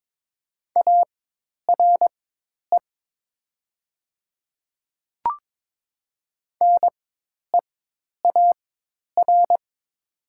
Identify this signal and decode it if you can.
Morse code, heard as a keyed tone, about 22 wpm, spaced as for 10 wpm, 700 Hz. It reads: ARE NEAR
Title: 4x Character Spacing - Top 200 Words - 22wpm